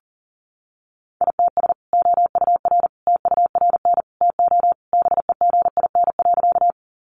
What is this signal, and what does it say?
Morse code, heard as a keyed tone, about 40 wpm, 700 Hz. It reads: ITS OUR TURN TO BEGIN.
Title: Sentences from Top 600 Words 40wpm